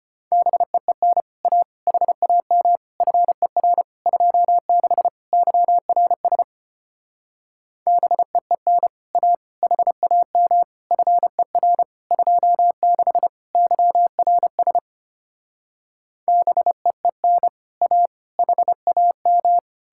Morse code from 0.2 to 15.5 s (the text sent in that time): BEEN A HAM FER 26 YRS BEEN A HAM FER 26 YRS